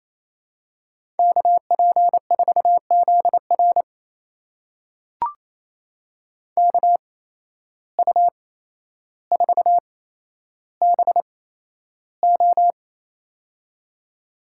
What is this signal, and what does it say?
Morse code, heard as a keyed tone, about 28 wpm, spaced as for 12 wpm, 700 Hz. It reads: KP4ZR KU4BO
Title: USA Call Signs - 5 Characters - 8x Character Spacing - 28wpm